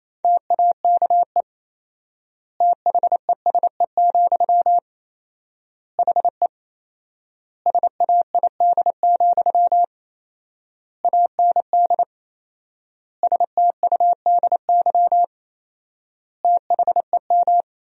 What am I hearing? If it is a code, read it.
Morse code, heard as a keyed tone, about 28 wpm, 700 Hz. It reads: TAKE THESE, HE SAID, AND STUDY THEM
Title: 4x Word-Spacing - Sentences from Top 300 Words - 28wpm